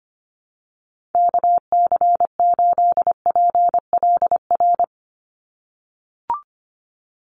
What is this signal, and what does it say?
Morse code, heard as a keyed tone, about 24 wpm, 700 Hz. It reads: KC8PLR
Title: USA Call Signs 25wpm